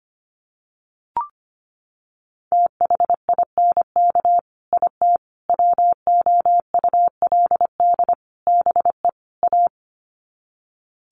Morse code, heard as a keyed tone, about 25 wpm, 700 Hz. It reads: THINK IT WOULD BE A
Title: Sets of 5 Words Encore2 25wpm